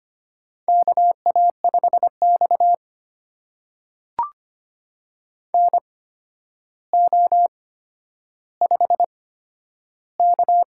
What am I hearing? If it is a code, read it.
Morse code, heard as a keyed tone, about 25 wpm, spaced as for 11 wpm, 700 Hz. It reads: KA5X NO5K
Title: USA Call Signs - 4 Characters - 8x Character Spacing - 25wpm